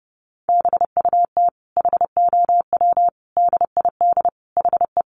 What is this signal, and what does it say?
Morse code, heard as a keyed tone, about 30 wpm, 700 Hz. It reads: BUT HOW DID HE
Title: Sentences from Top 900 Words 30wpm